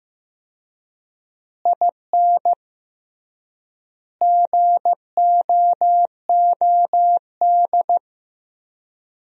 Morse code, heard as a keyed tone, about 15 wpm, 700 Hz. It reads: IN GOOD
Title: 3x Word-Spacing - Sentences from Top 200 Words - 15wpm